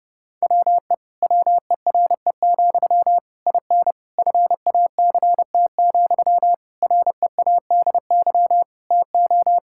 Morse code, heard as a keyed tone, about 30 wpm, 700 Hz. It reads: WE WERE, IN FACT, READY TO
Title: Sentences from Top 1000 Words - Review Part 5 - 30wpm